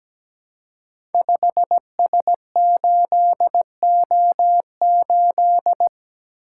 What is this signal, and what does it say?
Morse code, heard as a keyed tone, about 17 wpm, 700 Hz. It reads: 5S8O8
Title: Code Groups - 5 Letter:Numbers - 17wpm